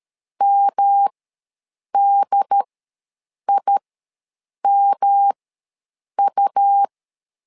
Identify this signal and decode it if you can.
Morse code, heard as a keyed tone, about 13 wpm, spaced as for 7 wpm, 800 Hz. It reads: MDIMU
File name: LEZIONI / 3 MWRO / 3 M ws 12wpm 1745